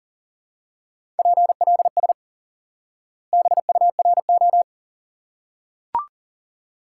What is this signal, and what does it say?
Morse code, heard as a keyed tone, about 37 wpm, 700 Hz. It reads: PLS BURO
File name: IQE Course 3 - Lesson 89 - YAGI - Two Elements - 40wpm